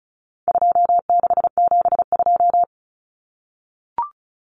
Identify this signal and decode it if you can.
Morse code, heard as a keyed tone, about 34 wpm, 700 Hz. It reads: 2672
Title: Numbers - 4-Digit 35wpm